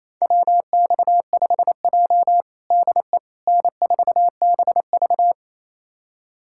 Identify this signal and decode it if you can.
Morse code, heard as a keyed tone, about 28 wpm, 700 Hz. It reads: WX5J DE N4BV
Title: USA Callsign DE Callsign - 4 Characters - 28wpm